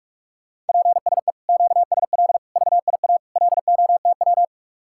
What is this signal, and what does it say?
Morse code, heard as a keyed tone, about 56 wpm, 700 Hz. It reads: PSE QSL VIA LOTW
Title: Speed-Racing Common QSO Phrases - 45wpm